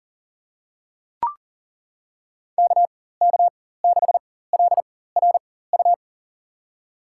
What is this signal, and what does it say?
Morse code, heard as a keyed tone, about 37 wpm, spaced as for 18 wpm, 700 Hz. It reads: KK6LRU
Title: USA Call Signs - 6 Characters - 4x Character Spacing - 40wpm